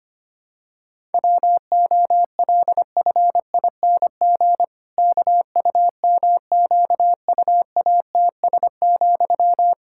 Morse code, heard as a keyed tone, about 25 wpm, 700 Hz. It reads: WOLFING KUMQUATS,
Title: Pangram Challenge v2 - 25wpm